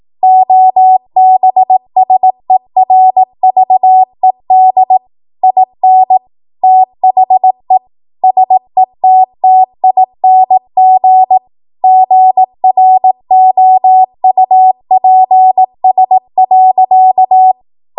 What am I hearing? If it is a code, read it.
Morse code, heard as a keyed tone, about 18 wpm, 750 Hz. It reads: OBSERVED IN THE SETTING GROUPS.
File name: Morse / 18WPM